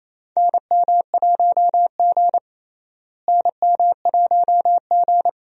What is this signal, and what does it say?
Morse code, heard as a keyed tone, about 28 wpm, 700 Hz. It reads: NM1G NM1G
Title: USA Call Signs - 4 Characters - Repeated 2x - 28wpm